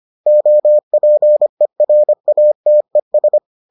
Morse code, heard as a keyed tone, about 25 wpm, 600 Hz. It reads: OPERATES